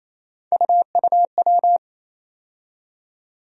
Morse code, heard as a keyed tone, about 28 wpm, 700 Hz. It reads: UUW
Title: USA Call Sign Postfixes - 3 Letters 28wpm